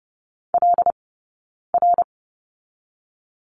Morse code, heard as a keyed tone, about 30 wpm, 700 Hz. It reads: L R
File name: 092 Lesson 16d - Letter c - Two Characters 30wpm